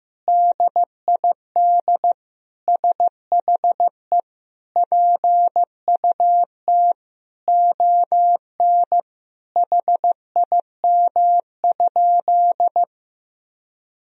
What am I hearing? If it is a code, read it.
Morse code, heard as a keyed tone, about 15 wpm, 700 Hz. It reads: DID SHE PUT ON HIM?
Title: Sentences from Top 1000 Words - Review Part 1 - 15wpm